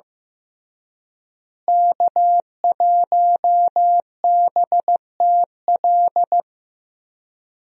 Morse code, heard as a keyed tone, about 16 wpm, 700 Hz. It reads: K1BTL
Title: USA Call Signs - 5 Characters - Repeated 2x - 15wpm